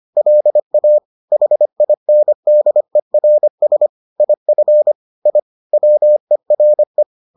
Morse code, heard as a keyed tone, about 25 wpm, 600 Hz. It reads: LA HINDERS IF I WERE